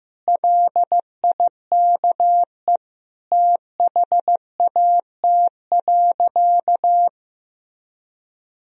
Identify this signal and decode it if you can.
Morse code, heard as a keyed tone, about 15 wpm, 700 Hz. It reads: LIKE THAT.
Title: Sentences from Top 1000 Words - Review Part 1 - 15wpm